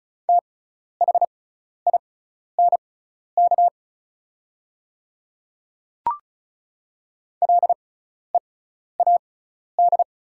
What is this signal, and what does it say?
Morse code, heard as a keyed tone, about 36 wpm, spaced as for 12 wpm, 690 Hz. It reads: THINK LEAD
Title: 6x Character Spacing - Top 500 Words - 35wpm